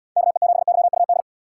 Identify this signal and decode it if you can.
Morse code, heard as a keyed tone, about 78 wpm, 700 Hz. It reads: KE7YUD